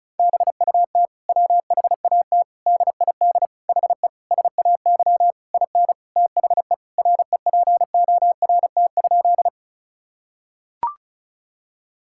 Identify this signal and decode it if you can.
Morse code, heard as a keyed tone, about 34 wpm, 700 Hz. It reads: BUT WHAT DID HE SAY IN THE REPORT?
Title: Sentences from Top 1000 Words - Review Part 6 - 35wpm